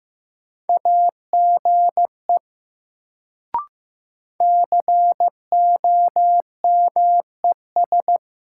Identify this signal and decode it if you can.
Morse code, heard as a keyed tone, about 15 wpm, 700 Hz. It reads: AGE COMES
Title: Sets of 3 Words - New Word Drill - 15wpm